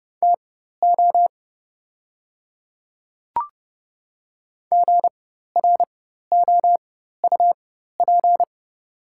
Morse code, heard as a keyed tone, about 30 wpm, spaced as for 14 wpm, 700 Hz. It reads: TO GROUP
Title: 4x Character Spacing - Top 300 Words - 30wpm